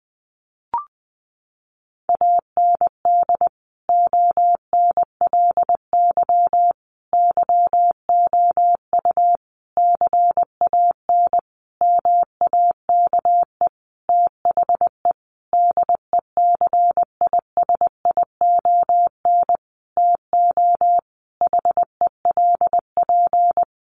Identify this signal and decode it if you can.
Morse code, heard as a keyed tone, about 19 wpm, 700 Hz. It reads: AND ONLY YOU CAN MAKE THE DECISION TO HELP
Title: Sentences from Top 1400 Words 20wpm